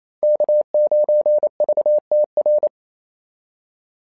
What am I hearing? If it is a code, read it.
Morse code, heard as a keyed tone, about 27 wpm, 660 Hz. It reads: K9VTR
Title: SOTA QSO Practice 28wpm